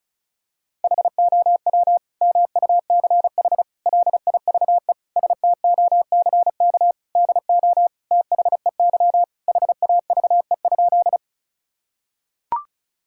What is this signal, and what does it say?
Morse code, heard as a keyed tone, about 36 wpm, 700 Hz. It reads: HOW MUCH LIVE STOCK DO THEY HAVE?
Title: Sentences from Top 1000 Words - Review Part 5 - 35wpm